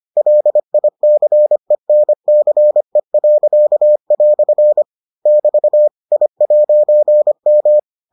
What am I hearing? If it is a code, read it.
Morse code, heard as a keyed tone, about 25 wpm, 600 Hz. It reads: LICENCE." <BT> I'M